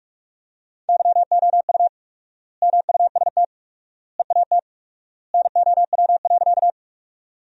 Morse code, heard as a keyed tone, about 45 wpm, 700 Hz. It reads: YOU MUST EAT NOW.
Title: 4x Word-Spacing - Sentences from Top 300 Words - 45wpm